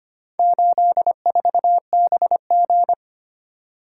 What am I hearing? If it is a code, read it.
Morse code, heard as a keyed tone, about 25 wpm, 700 Hz. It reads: 84BG
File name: practice / 25wpm / Four Letter-Number Combinations - 25wpm